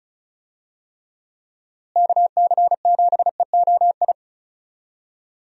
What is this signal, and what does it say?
Morse code, heard as a keyed tone, about 34 wpm, 700 Hz. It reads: KC7EOI
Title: USA Call Signs - 6 Characters - Repeated 2x - 35wpm